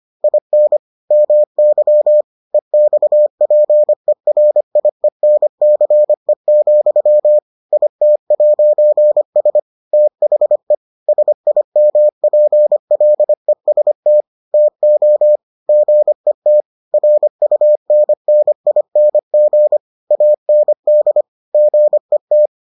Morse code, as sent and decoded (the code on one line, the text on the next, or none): .. -. / -- -.-- / . -..- .--. .-. .. -. -.-. --..-- / .. - .----. ... / - .... / ... .. -- .--. .-.. ... - / - --- / --. - / .-. ..- -. -. .. -. --. / .- -. -.. / --. -
IN MY EXPERIENCE, IT'S THE SIMPLEST TO GET RUNNING AND GET